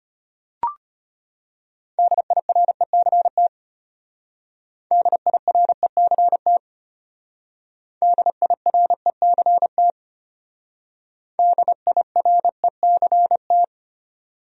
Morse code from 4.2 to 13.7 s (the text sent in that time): DIRECT DIRECT DIRECT